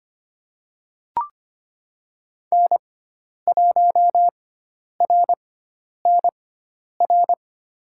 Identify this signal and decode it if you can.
Morse code, heard as a keyed tone, about 25 wpm, spaced as for 10 wpm, 700 Hz. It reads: N1RNR
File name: Lesson 16g - Letter c - Call Signs 25wpm